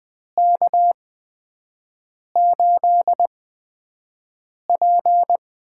Morse code, heard as a keyed tone, about 20 wpm, spaced as for 5 wpm, 700 Hz. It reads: K8P
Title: USA Call Signs - 4 Characters - 8x Character Spacing - 20wpm